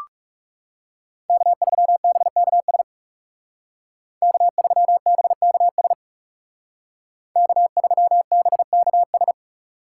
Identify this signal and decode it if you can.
Morse code, heard as a keyed tone, about 36 wpm, 700 Hz. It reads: K3BKS K3BKS K3BKS
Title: Speed Racing - USA 1x3 Call Signs 30wpm